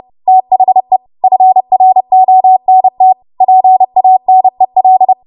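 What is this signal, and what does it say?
Morse code, heard as a keyed tone, about 30 wpm, 750 Hz. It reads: THE FRONT PANEL
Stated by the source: Morse practice transmission at 30 WPM.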